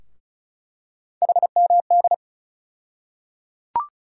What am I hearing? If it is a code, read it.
Morse code, heard as a keyed tone, about 36 wpm, 700 Hz. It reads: HMD
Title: USA Call Sign Postfixes - 3 Letters 35wpm